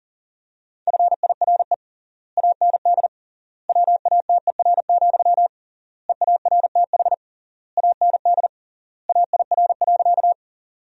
Morse code, heard as a keyed tone, about 40 wpm, 700 Hz. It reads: FIRE AND WATER, EARTH AND AIR.
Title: 3x Word-Spacing - Sentences from Top 400 Words - 40wpm